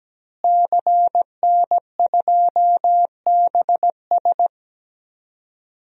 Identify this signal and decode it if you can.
Morse code, heard as a keyed tone, about 17 wpm, 700 Hz. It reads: CN2BS
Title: DXCC-100 Call Signs - 5 Characters - Repeated 2x - 17wpm